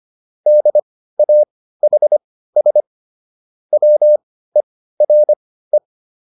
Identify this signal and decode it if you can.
Morse code, heard as a keyed tone, about 25 wpm, spaced as for 15 wpm, 600 Hz. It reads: DAHS WERE
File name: SS215 15